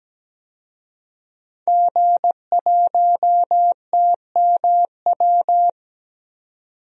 Morse code, heard as a keyed tone, about 17 wpm, 700 Hz. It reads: G1TMW
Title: DXCC-100 Call Signs - 5 Characters - 17wpm